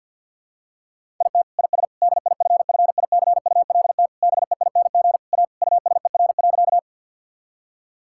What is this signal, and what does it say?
Morse code, heard as a keyed tone, about 50 wpm, 700 Hz. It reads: IT IS DIFFICULT BEING A USER.
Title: Sentences from Top 1000 Words - Review Part 5 - 50wpm